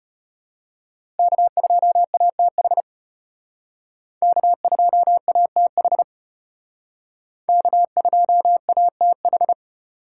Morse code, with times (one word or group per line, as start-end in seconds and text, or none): 1.2-2.8 K2ATH
4.2-6.0 K2ATH
7.5-9.5 K2ATH